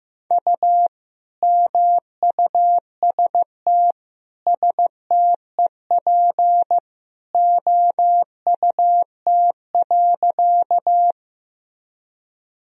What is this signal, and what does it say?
Morse code, heard as a keyed tone, about 14 wpm, 700 Hz. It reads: U MUST STEP OUT.
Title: Sentences from Top 400 Words and Abbreviations 15wpm